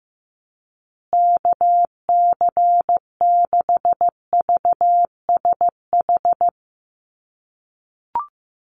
Morse code, heard as a keyed tone, about 15 wpm, 700 Hz. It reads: KC6VSH
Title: USA Call Signs Repeated 2x 15wpm